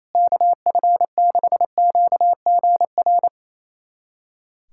Morse code, heard as a keyed tone, about 28 wpm, 700 Hz. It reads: KF6QGR
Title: USA Callsign DE Callsign 28wpm